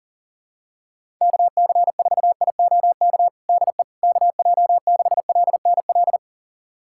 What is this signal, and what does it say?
Morse code, heard as a keyed tone, about 40 wpm, 700 Hz. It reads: KC4IOK DE KJ6LNL
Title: USA Callsign DE Callsign - 6 Characters - 40wpm